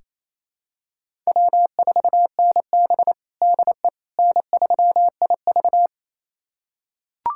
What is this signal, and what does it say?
Morse code, heard as a keyed tone, about 28 wpm, 700 Hz. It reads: W4NB DE N3IV
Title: USA Callsign DE Callsign - 4 Characters - 28wpm